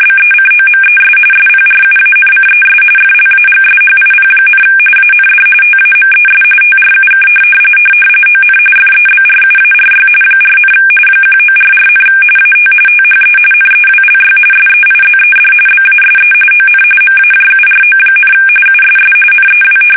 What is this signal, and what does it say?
RTTY, about 45.45 baud. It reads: BECOMES SPRUANCES FLAGSHIP, AND TAMES TRUK ON FEBRUARY 4, 1944, AT THE NEWLY SEIZED BASE ON MAJURO, USS NEW
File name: fsk850 / battleshipnj